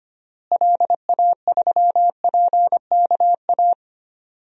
Morse code, heard as a keyed tone, about 25 wpm, 700 Hz. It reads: LA3PKA
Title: DXCC-100 Call Signs - 6 Characters - Repeated 2x - 25wpm